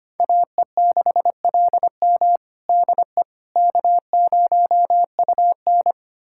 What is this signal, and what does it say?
Morse code, heard as a keyed tone, about 25 wpm, 700 Hz. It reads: AE6LM DE K0UN
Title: USA Callsign DE Callsign 25wpm